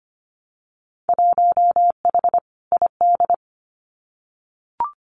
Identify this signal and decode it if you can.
Morse code, heard as a keyed tone, about 25 wpm, 700 Hz. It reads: 1H ID
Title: Winter Field Day Exchanges 25wpm